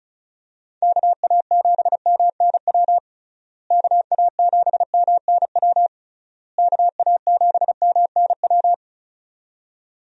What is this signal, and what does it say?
Morse code, heard as a keyed tone, about 34 wpm, 700 Hz. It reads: KA7MNW KA7MNW KA7MNW
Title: USA Call Signs - 6 Characters - Repeated 3x - 35wpm